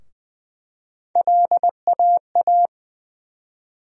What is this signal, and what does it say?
Morse code, heard as a keyed tone, about 20 wpm, 700 Hz. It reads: LAA
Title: USA Call Sign Postfixes - 3 Letters 20wpm